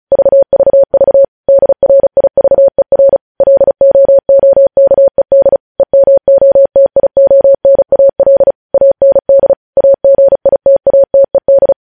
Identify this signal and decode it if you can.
Morse code, heard as a keyed tone, about 36 wpm, 550 Hz. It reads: VVV DRIVER LOOKED EMOTIONAL AND AGITATED
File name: podcast / News headlines-1x-35x35